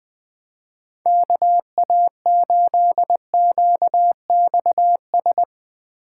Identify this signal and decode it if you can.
Morse code, heard as a keyed tone, about 21 wpm, 700 Hz. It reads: KA8QXS